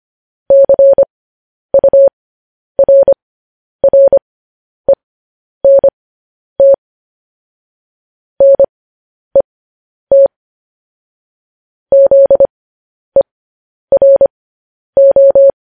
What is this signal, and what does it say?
Morse code, heard as a keyed tone, about 25 wpm, spaced as for 10 wpm, 550 Hz. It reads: CURRENT NET ZERO
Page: news headlines in morse code at 10 wpm